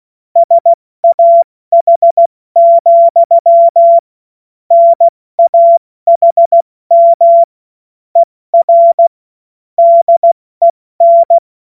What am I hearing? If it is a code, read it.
Morse code, heard as a keyed tone, about 16 wpm, spaced as for 14 wpm, 680 Hz. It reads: SAH, NAHM ER DEN
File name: Morsekurs / Witze / Witz 10. Mai 2022 70 F80